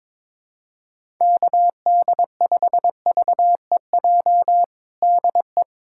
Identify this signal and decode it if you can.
Morse code, heard as a keyed tone, about 22 wpm, 700 Hz. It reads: KD5VEJ DE
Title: USA Callsign DE Callsign - 6 Characters - 22wpm